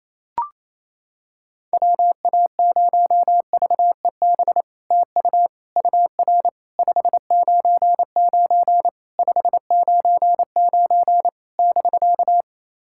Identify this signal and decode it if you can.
Morse code, heard as a keyed tone, about 28 wpm, 700 Hz. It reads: WA0VEB TU UR 599 599 <BK>
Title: SOTA QSO Practice 28wpm